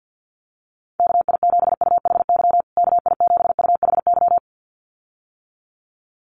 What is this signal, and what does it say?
Morse code, heard as a keyed tone, about 49 wpm, 700 Hz. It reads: KI7UHY KI7UHY
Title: USA Call Signs Repeated 2x 50wpm